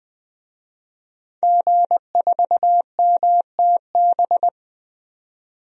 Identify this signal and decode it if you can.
Morse code, heard as a keyed tone, about 21 wpm, 700 Hz. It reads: G4MTB
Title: DXCC-100 Call Signs - 5 Characters - 20wpm